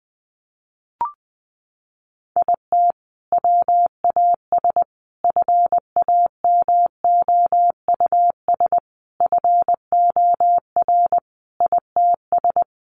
Morse code, heard as a keyed tone, about 20 wpm, 700 Hz. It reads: IT WAS FAMOUS FOR ITS
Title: Sentences from Top 1500 Words 20wpm